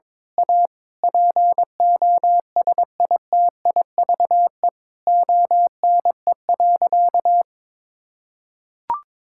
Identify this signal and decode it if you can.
Morse code, heard as a keyed tone, about 22 wpm, 700 Hz. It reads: A POSITIVE ONE.
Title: Sentences from Top 1000 Words - Review Part 6 - 22wpm